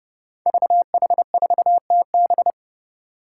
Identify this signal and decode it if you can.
Morse code, heard as a keyed tone, about 29 wpm, 700 Hz. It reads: VH4TB
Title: Five Challenging Letter-Number Combinations 30wpm